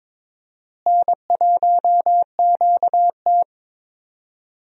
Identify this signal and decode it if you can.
Morse code, heard as a keyed tone, about 22 wpm, 700 Hz. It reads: N1QT
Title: USA Call Signs - 4 Characters - Repeated 2x - 22wpm